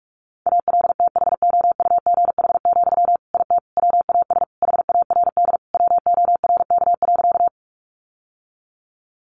Morse code, heard as a keyed tone, about 44 wpm, 700 Hz. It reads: ALTHOUGH, IT WAS HARD WORK.
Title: Sentences from Top 600 Words 45wpm